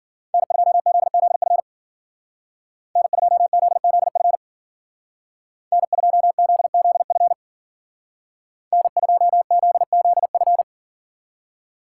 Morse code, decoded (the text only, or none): N2ZZF N2ZZF N2ZZF N2ZZF